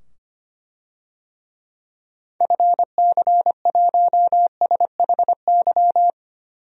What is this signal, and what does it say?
Morse code, heard as a keyed tone, about 25 wpm, 700 Hz. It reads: FC1SHY
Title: DXCC-100 Call Signs - 6 Characters - 25wpm